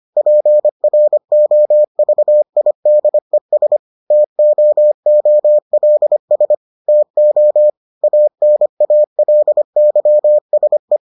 Morse code, heard as a keyed tone, about 25 wpm, 600 Hz. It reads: PROVIDES TOOLS TO ANALYSE